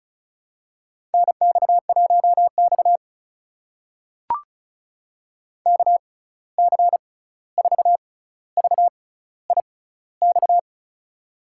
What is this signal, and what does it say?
Morse code, heard as a keyed tone, about 36 wpm, spaced as for 16 wpm, 700 Hz. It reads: NX1X KC4VIX
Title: USA Call Signs - 6x Character Spacing - 35wpm